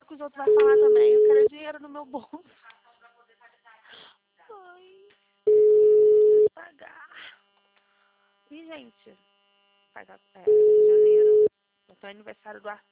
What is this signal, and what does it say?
Ringback tone (Europe, 425 Hz), with speech.
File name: out- -2001- - - .395